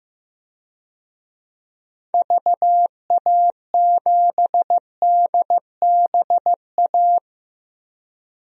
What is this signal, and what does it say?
Morse code, heard as a keyed tone, about 15 wpm, 700 Hz. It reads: VA7DBA
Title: DXCC-100 Call Signs - 6 Characters - 15wpm